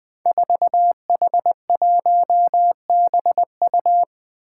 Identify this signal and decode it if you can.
Morse code, heard as a keyed tone, about 20 wpm, 700 Hz. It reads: 4H1BU
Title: Five Challenging Letter-Number Combinations 20wpm